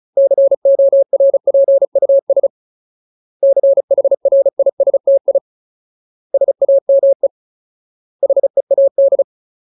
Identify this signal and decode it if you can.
Morse code, heard as a keyed tone, about 34 wpm, 550 Hz. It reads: CORPUS CHRISTI SAME HEAD
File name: Ultimate Word List / 35WPM Ultimate